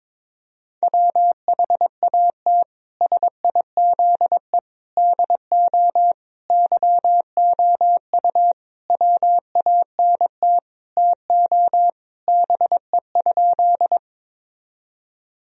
Morse code, heard as a keyed tone, about 21 wpm, 700 Hz. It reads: WHAT SIZE DO YOU WANT TO BE?
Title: Sentences from Top 1000 Words - Review Part 2 - 22wpm